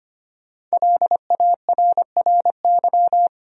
Morse code, heard as a keyed tone, about 25 wpm, 700 Hz. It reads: LARRY
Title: USA Common Names - 25wpm